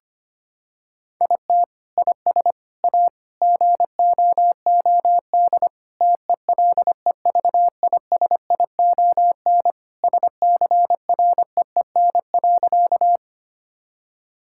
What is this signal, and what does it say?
Morse code, heard as a keyed tone, about 25 wpm, 700 Hz. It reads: IT IS A GOOD TELEVISION SCREEN.
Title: Sentences from Top 1000 Words - Review Part 6 - 25wpm